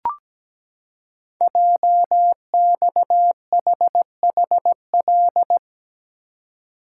Morse code, heard as a keyed tone, about 17 wpm, 700 Hz. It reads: JXHHL
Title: Code Groups - 5 Letters - 17wpm